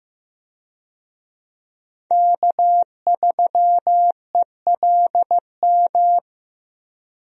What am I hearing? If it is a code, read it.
Morse code, heard as a keyed tone, about 16 wpm, 700 Hz. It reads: K3ELM
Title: USA Call Signs - 5 Characters - Repeated 2x - 15wpm